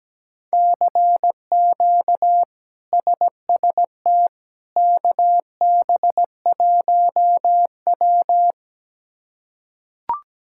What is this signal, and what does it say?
Morse code, heard as a keyed tone, about 17 wpm, 700 Hz. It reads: CQ SST KB1W
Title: K1USN SST QSO Practice - 17wpm